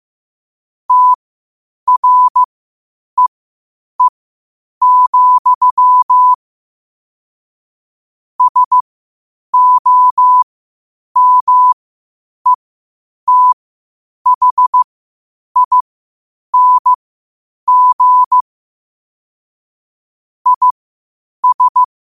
Morse code, heard as a keyed tone, about 15 wpm, spaced as for 8 wpm, 1000 Hz. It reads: TREE, SOMETHING IS
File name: qotdfiles / 8wpm